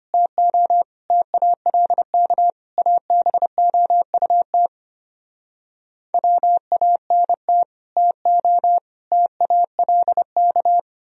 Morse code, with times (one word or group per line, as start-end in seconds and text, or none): 0.1-0.8 TO
1.1-2.5 TALK
2.8-4.7 ABOUT
6.1-7.6 WANT
8.0-8.8 TO
9.1-10.8 TALK